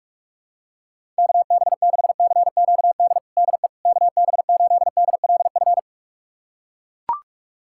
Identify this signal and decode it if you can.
Morse code, heard as a keyed tone, about 43 wpm, 700 Hz. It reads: KB6CQD DE KB8DLF
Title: USA Callsign DE Callsign 45wpm